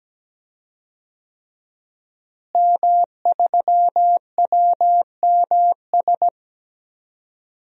Morse code, heard as a keyed tone, about 17 wpm, 700 Hz. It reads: M3WMS
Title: DXCC-100 Call Signs - 5 Characters - Repeated 3x - 17wpm